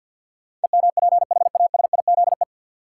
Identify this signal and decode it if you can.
Morse code, heard as a keyed tone, about 52 wpm, 700 Hz. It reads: EMPHASIZE
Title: Sentences from Top 1500 Words - New Word Drill - 50wpm